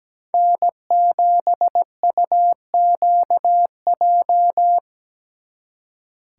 Morse code, heard as a keyed tone, about 20 wpm, 700 Hz. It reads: N7UQJ
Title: Speed Racing - USA Call Signs 17wpm